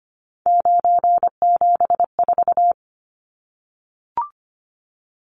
Morse code, heard as a keyed tone, about 25 wpm, 700 Hz. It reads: 974
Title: Numbers - 3-Digit 25wpm